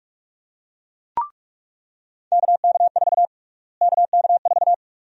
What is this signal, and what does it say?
Morse code, heard as a keyed tone, about 42 wpm, 700 Hz. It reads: KK4 KK4
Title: USA Call Sign Prefixes - Repeated 2x - 45wpm